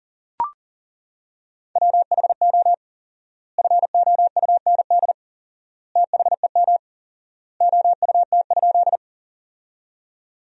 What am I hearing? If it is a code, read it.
Morse code, heard as a keyed tone, about 40 wpm, 700 Hz. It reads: WHO FOUND THEM OUT?
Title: 4x Word-Spacing - Sentences from Top 300 Words - 40wpm